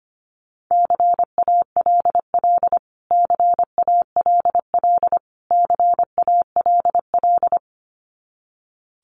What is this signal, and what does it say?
Morse code, heard as a keyed tone, about 25 wpm, 700 Hz. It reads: CALL CALL CALL
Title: Top 300 Words Repeated 25wpm